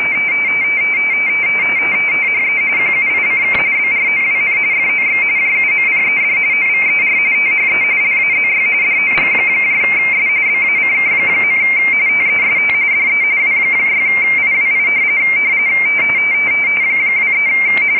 RTTY, about 45.45 baud. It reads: THE QUICK BROWN FOX JUMPED OVER THE LAZY DOGS BACK. TEST DE W6IWI 23
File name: rtty / websdr recording 14086.0kHz